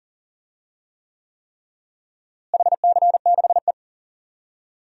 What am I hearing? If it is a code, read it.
Morse code, heard as a keyed tone, about 40 wpm, 700 Hz. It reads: HC6E
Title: DXCC-100 Call Signs - 4 Characters - Repeated 3x - 40wpm